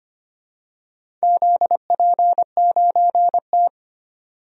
Morse code, heard as a keyed tone, about 25 wpm, 700 Hz. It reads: ZP9T
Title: DXCC-100 Call Signs - 4 Characters - 25wpm